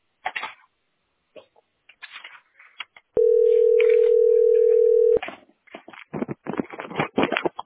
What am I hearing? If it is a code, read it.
Ringback tone (North America), 440 + 480 Hz.